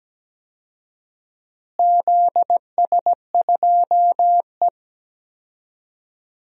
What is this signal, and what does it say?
Morse code, heard as a keyed tone, about 17 wpm, 700 Hz. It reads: ZS2E
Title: DXCC-100 Call Signs - 4 Characters - Repeated 2x - 17wpm